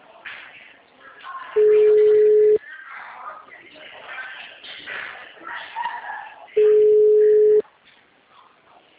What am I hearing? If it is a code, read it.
Ringback tone (Europe, 425 Hz), with speech.
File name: out- -2015- - - .54814